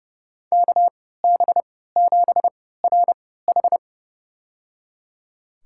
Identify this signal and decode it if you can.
Morse code, heard as a keyed tone, about 30 wpm, spaced as for 17 wpm, 700 Hz. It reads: KB7RH
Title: USA Call Signs - 3x Character Spacing - 30wpm